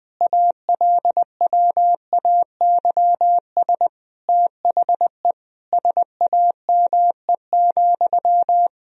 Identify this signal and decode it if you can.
Morse code, heard as a keyed tone, about 19 wpm, 700 Hz. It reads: ALWAYS THE SAME,
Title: Sentences from Top 1000 Words - Review Part 1 - 20wpm